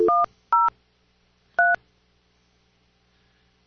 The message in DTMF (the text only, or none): 103